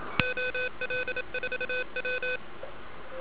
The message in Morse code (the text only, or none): OL4W